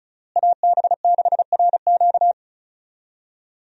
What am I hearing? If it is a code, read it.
Morse code, heard as a keyed tone, about 36 wpm, 700 Hz. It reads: AB6RQ
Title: USA Call Signs - 5 Characters - Repeated 3x - 35wpm